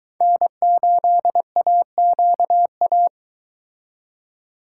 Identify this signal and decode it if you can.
Morse code, heard as a keyed tone, about 24 wpm, 700 Hz. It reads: N8AQA